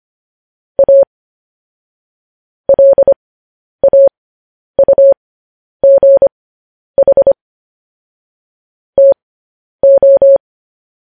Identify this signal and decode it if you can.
Morse code, heard as a keyed tone, about 25 wpm, spaced as for 10 wpm, 550 Hz. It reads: A LAUGH TO
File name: podcast / News headlines-1x-25x10